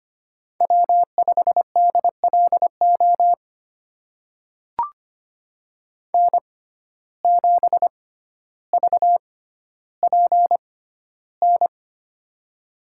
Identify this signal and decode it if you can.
Morse code, heard as a keyed tone, about 25 wpm, spaced as for 13 wpm, 700 Hz. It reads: W5DLO N7VPN
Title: USA Call Signs - 5 Characters - 6x Character Spacing - 25wpm